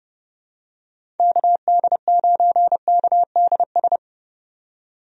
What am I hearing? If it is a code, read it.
Morse code, heard as a keyed tone, about 30 wpm, 700 Hz. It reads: KD9KDS
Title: USA Call Signs - 6 Characters - 30wpm